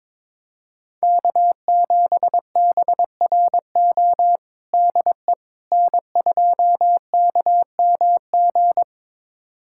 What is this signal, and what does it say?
Morse code, heard as a keyed tone, about 22 wpm, 700 Hz. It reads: K7BRO DE N2KMG
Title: USA Callsign DE Callsign - 5 Characters - 22wpm